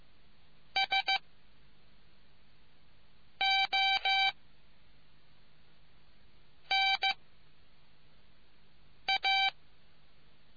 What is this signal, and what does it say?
Morse code, heard as a keyed tone, about 15 wpm, 780 Hz. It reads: S O N A